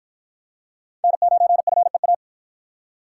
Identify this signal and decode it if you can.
Morse code, heard as a keyed tone, about 47 wpm, 700 Hz. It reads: N9FEA